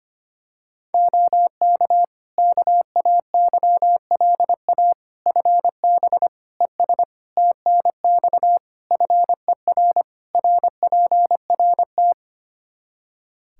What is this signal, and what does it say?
Morse code, heard as a keyed tone, about 25 wpm, 700 Hz. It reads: OK KAYLA FB ES TNX FER RPRT
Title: K1USN SST QSO Practice 25wpm